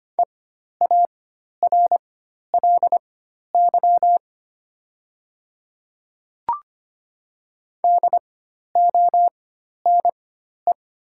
Morse code, heard as a keyed tone, about 25 wpm, spaced as for 12 wpm, 700 Hz. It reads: EARLY DONE
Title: 4x Character Spacing - Top 500 Words - 25wpm